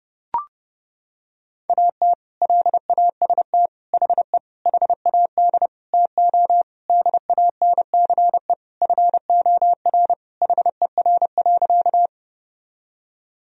Morse code, heard as a keyed tone, about 32 wpm, 700 Hz. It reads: AT LAST HE HAD TO DANCE FOR HER.
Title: Sentences from Top 1000 Words - Review Part 4 - 30wpm